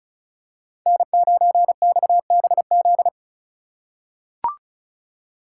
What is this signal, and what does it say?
Morse code, heard as a keyed tone, about 36 wpm, 700 Hz. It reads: N9XBZ
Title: USA Call Signs - 5 Characters - 6x Character Spacing - 35wpm